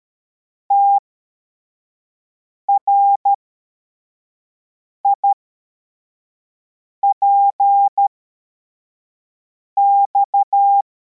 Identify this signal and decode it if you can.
Morse code, heard as a keyed tone, about 13 wpm, spaced as for 4 wpm, 800 Hz. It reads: TRIPX
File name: formation / cw / Text85